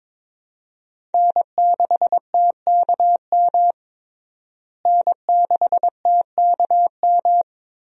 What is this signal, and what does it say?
Morse code, heard as a keyed tone, about 22 wpm, 700 Hz. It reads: N6TKM N6TKM
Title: USA Call Signs - 5 Characters - Repeated 2x - 22wpm